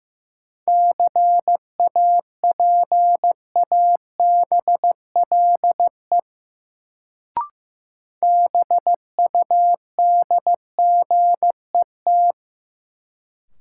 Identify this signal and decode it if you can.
Morse code, heard as a keyed tone, about 15 wpm, 700 Hz. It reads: CAPABLE BUDGET
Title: Sentences from Top 1200 Words - New Word Drill - 15wpm